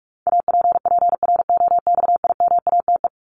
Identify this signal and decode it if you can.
Morse code, heard as a keyed tone, about 44 wpm, 700 Hz. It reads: APPROXIMATE
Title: LY Suffix - Top 100 Words with Base Words 45wpm